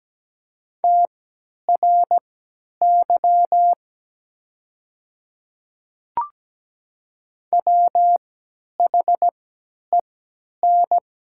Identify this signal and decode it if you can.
Morse code, heard as a keyed tone, about 17 wpm, spaced as for 10 wpm, 700 Hz. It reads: TRY WHEN
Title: 3x Character Spacing - Top 200 Words - 17wpm